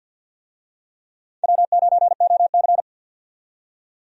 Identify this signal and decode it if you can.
Morse code, heard as a keyed tone, about 52 wpm, 700 Hz. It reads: W9OC
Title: USA Call Signs - 4 Characters - 50wpm